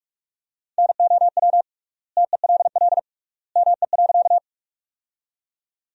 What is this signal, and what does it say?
Morse code, heard as a keyed tone, about 46 wpm, 700 Hz. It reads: NOW TELL ME.
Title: 3x Word-Spacing - Sentences from Top 200 Words - 45wpm